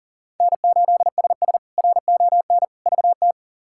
Morse code, heard as a keyed tone, about 40 wpm, 700 Hz. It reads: N8SS RON VT
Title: K1USN SST QSO Practice - 40wpm